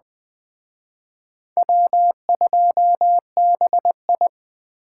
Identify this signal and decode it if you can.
Morse code, heard as a keyed tone, about 20 wpm, 700 Hz. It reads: W2BI